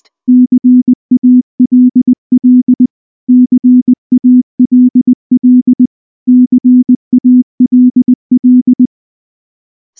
Morse code, heard as a keyed tone, about 19 wpm, 250 Hz. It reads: CALL CALL CALL
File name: morse / morse-training-20WPM-2